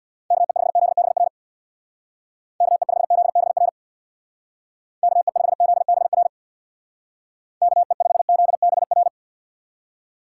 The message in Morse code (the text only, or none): KE5ZBR KE5ZBR KE5ZBR KE5ZBR